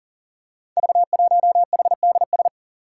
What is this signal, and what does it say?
Morse code, heard as a keyed tone, about 40 wpm, 700 Hz. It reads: V1HDS
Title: Five Challenging Letter-Number Combinations 40wpm